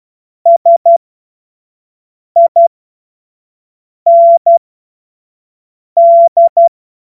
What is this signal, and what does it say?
Morse code, heard as a keyed tone, about 12 wpm, spaced as for 5 wpm, 680 Hz. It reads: SIND
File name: Morsekurs / Witze / Witz 07. Juni 2022 25 F70